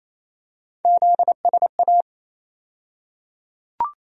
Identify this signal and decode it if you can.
Morse code, heard as a keyed tone, about 27 wpm, 700 Hz. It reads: ZSA
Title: USA Call Sign Postfixes - 3 Letters 28wpm